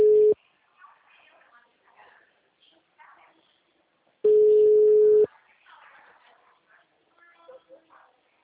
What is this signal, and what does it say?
Ringback tone (Europe, 425 Hz), with speech.